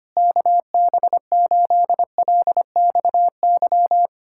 Morse code, heard as a keyed tone, about 26 wpm, 700 Hz. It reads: KB8LXY
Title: Speed-Racing Common QSO Phrases - 25wpm